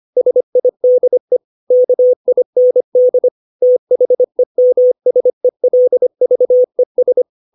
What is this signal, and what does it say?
Morse code, heard as a keyed tone, about 25 wpm, 500 Hz. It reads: SIDE KIND THEMSELVES
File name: 632 words / 632 Words 25WPM